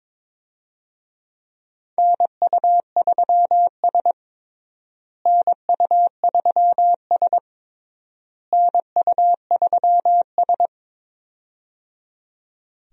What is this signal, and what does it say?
Morse code, heard as a keyed tone, about 22 wpm, 700 Hz. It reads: NU3S NU3S NU3S
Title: USA Call Signs - 4 Characters - Repeated 3x - 22wpm